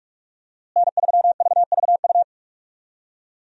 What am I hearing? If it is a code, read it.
Morse code, heard as a keyed tone, about 46 wpm, 700 Hz. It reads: N3VVU
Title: USA Call Signs - 5 Characters - Repeated 3x - 45wpm